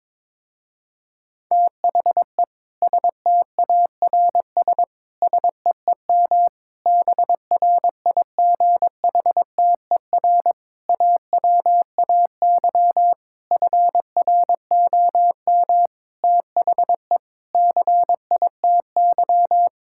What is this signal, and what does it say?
Morse code, heard as a keyed tone, about 22 wpm, 700 Hz. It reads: THE STARS SEEM BRIGHTER AWAY FROM THE CITY
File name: Difficult Sentences with Hints - 22wpm